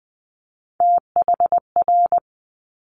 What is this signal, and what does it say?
Morse code, heard as a keyed tone, about 20 wpm, 700 Hz. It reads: THR
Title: Multi-letter Phonemes Quick Fire 20wpm